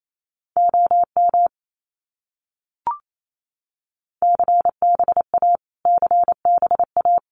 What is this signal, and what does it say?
Morse code, heard as a keyed tone, about 28 wpm, 700 Hz. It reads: OM CBA CBA
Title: Abbreviations Repeated 3x - 28wpm